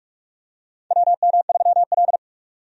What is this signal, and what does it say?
Morse code, heard as a keyed tone, about 46 wpm, 700 Hz. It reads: WM3L
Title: USA Call Signs - 4 Characters - 45wpm